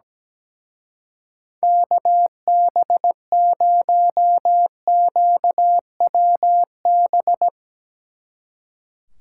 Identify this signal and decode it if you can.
Morse code, heard as a keyed tone, about 17 wpm, 700 Hz. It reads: KB0QWB
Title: Speed Racing - USA 2x3 Call Signs 17wpm